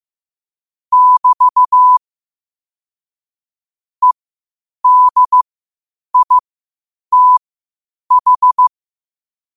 Morse code, heard as a keyed tone, about 15 wpm, spaced as for 8 wpm, 1000 Hz. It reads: <BT> EDITH
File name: qotdfiles / 8wpm